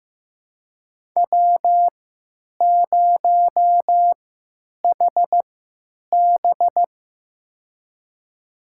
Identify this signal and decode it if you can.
Morse code, heard as a keyed tone, about 15 wpm, spaced as for 9 wpm, 700 Hz. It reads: W0HB
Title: USA Call Signs - 3x Character Spacing - 15wpm